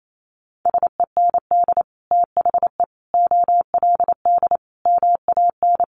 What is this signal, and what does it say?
Morse code, heard as a keyed tone, about 28 wpm, 700 Hz. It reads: SEND THE OLD MAN
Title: Sentences from Top 1500 Words 28wpm